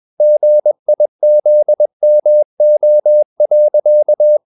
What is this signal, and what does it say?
Morse code, heard as a keyed tone, about 21 wpm, 600 Hz. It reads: GIZMO.
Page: CW med Gnister